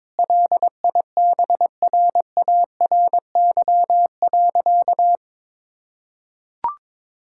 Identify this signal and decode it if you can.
Morse code, heard as a keyed tone, about 22 wpm, 700 Hz. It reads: LIBRARY.
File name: Difficult Sentences with Hints - 22wpm